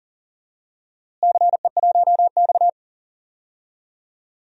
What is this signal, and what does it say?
Morse code, heard as a keyed tone, about 40 wpm, 700 Hz. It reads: CE1X
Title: DXCC-100 Call Signs - 4 Characters - Repeated 2x - 40wpm